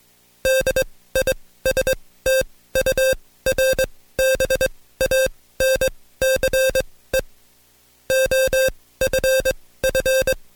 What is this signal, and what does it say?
Morse code, heard as a keyed tone, about 21 wpm, spaced as for 16 wpm, 530 Hz. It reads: DISTURBANCE OFF
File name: Code Quick / AudioDownld2 / Seg134-Stepper7 15WPM